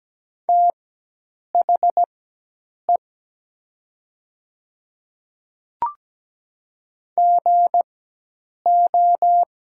Morse code, heard as a keyed tone, about 17 wpm, spaced as for 8 wpm, 700 Hz. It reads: THE GO
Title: 4x Character Spacing - Top 100 Words - 17wpm